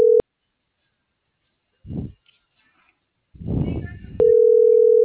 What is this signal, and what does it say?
Ringback tone (North America), 440 + 480 Hz.